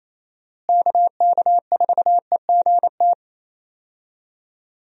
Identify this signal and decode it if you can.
Morse code, heard as a keyed tone, about 27 wpm, 700 Hz. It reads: KK4EGT
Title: USA Call Signs - 6 Characters - Repeated 3x - 28wpm